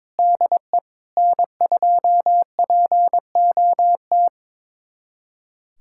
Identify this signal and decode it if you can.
Morse code, heard as a keyed tone, about 22 wpm, 700 Hz. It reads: DE N2POT
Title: USA Callsign DE Callsign - 5 Characters - 22wpm